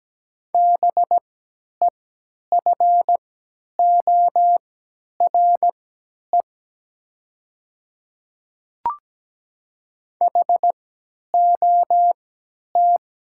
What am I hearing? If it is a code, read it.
Morse code, heard as a keyed tone, about 17 wpm, spaced as for 10 wpm, 700 Hz. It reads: BEFORE HOT
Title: 3x Character Spacing - Top 200 Words - 17wpm